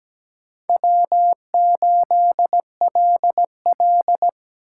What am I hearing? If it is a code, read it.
Morse code, heard as a keyed tone, about 20 wpm, 700 Hz. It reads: W8LL
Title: Speed Racing - USA 1x2 Call Signs 17wpm